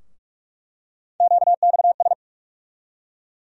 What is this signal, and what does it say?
Morse code, heard as a keyed tone, about 46 wpm, 700 Hz. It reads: QXS
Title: USA Call Sign Postfixes - 3 Letters 45wpm